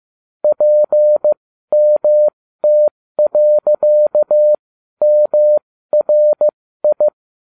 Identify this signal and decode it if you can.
Morse code, heard as a keyed tone, about 15 wpm, spaced as for 12 wpm, 600 Hz. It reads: PMT.MRI
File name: cw / ubacw20 3